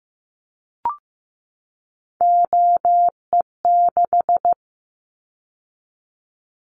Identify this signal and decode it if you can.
Morse code, heard as a keyed tone, about 15 wpm, 700 Hz. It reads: OE6
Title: Callsign Prefixes for Top 50 DXCC - 15wpm